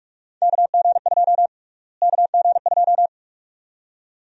Morse code, heard as a keyed tone, about 46 wpm, 690 Hz. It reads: KG2 KG2
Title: USA Call Sign Prefixes - Repeated 2x - 45wpm